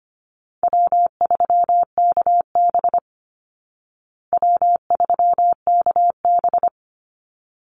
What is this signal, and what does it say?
Morse code, heard as a keyed tone, about 25 wpm, 700 Hz. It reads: W3KB W3KB
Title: Easy Call Signs Repeated 2x - 25wpm